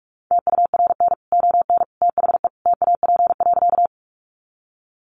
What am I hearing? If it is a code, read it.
Morse code, heard as a keyed tone, about 44 wpm, 700 Hz. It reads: TURN ON THE TAP.
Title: Sentences from Top 1400 Words 45wpm